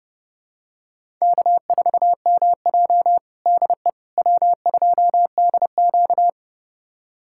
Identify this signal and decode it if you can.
Morse code, heard as a keyed tone, about 32 wpm, 700 Hz. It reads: K4MJ DE W2DQ
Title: USA Callsign DE Callsign - 4 Characters - 30wpm